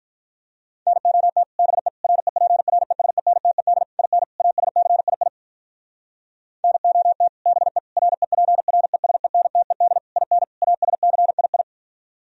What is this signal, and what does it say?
Morse code, heard as a keyed tone, about 44 wpm, 700 Hz. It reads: NOT BE REPRESENTED IN ASCII NOT BE REPRESENTED IN ASCII